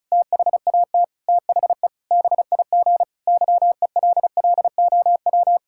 Morse code, heard as a keyed tone, about 34 wpm, 700 Hz. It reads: THAT THE BIG YELLOW
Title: Pangram Challenge v2 - 35wpm